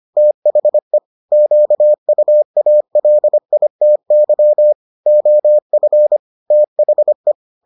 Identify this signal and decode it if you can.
Morse code, heard as a keyed tone, about 25 wpm, 600 Hz. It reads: THE QUALITY OF THE